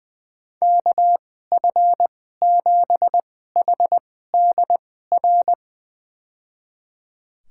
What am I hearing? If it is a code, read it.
Morse code, heard as a keyed tone, about 20 wpm, spaced as for 14 wpm, 700 Hz. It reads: KF7HDR
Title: USA Call Signs - 6 Characters - 2x Character Spacing - 20wpm